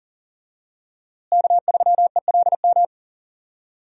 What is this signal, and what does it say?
Morse code, heard as a keyed tone, about 39 wpm, 700 Hz. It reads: K3ELM
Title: USA Call Signs - 5 Characters - Repeated 2x - 40wpm